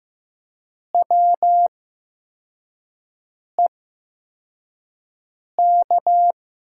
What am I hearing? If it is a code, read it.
Morse code, heard as a keyed tone, about 15 wpm, spaced as for 4 wpm, 700 Hz. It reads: WEK
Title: USA Call Signs - 5 Characters - 8x Character Spacing - 15wpm